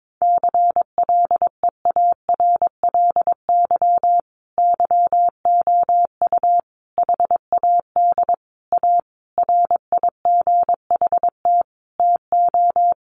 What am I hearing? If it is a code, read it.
Morse code, heard as a keyed tone, about 22 wpm, 700 Hz. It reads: CLEARLY YOU HAD A RIGHT TO
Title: Sentences from Top 1300 Words 22wpm